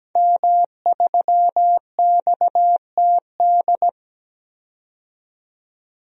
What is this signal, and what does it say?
Morse code, heard as a keyed tone, about 17 wpm, 700 Hz. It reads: M3XTD
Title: DXCC-100 Call Signs - 5 Characters - 17wpm